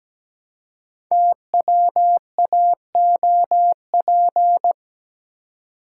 Morse code, heard as a keyed tone, about 17 wpm, 700 Hz. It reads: TWAOP
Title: Code Groups - 5 Letters - 17wpm